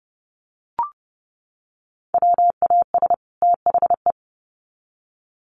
Morse code, heard as a keyed tone, about 30 wpm, 700 Hz. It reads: WAS THE
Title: Sets of 2 Words 30wpm